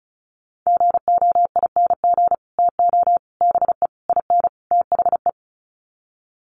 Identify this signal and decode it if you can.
Morse code, heard as a keyed tone, about 34 wpm, 700 Hz. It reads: GOING TO BE IN THE
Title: Sets of 5 Words Encore 35wpm